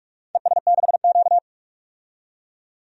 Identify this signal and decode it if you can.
Morse code, heard as a keyed tone, about 46 wpm, 700 Hz. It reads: ES6Q
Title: DXCC-100 Call Signs - 4 Characters - 45wpm